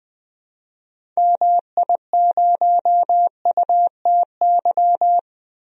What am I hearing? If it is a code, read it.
Morse code, heard as a keyed tone, about 20 wpm, 700 Hz. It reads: MI0UTY
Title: DXCC-100 Call Signs - 6 Characters - 20wpm